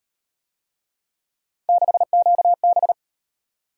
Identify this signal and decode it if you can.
Morse code, heard as a keyed tone, about 38 wpm, 700 Hz. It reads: 6QB